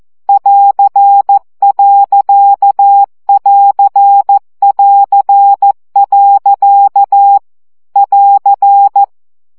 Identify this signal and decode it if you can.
Morse code, heard as a keyed tone, about 14 wpm, 800 Hz. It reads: <AR>.<AR><AR>. <AR>
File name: CW gyakorlas a2 plusz pont 12 baud